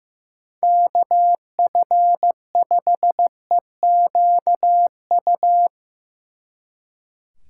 Morse code, heard as a keyed tone, about 17 wpm, 700 Hz. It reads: KF5EQU
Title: Speed Racing - USA 2x3 Call Signs 15wpm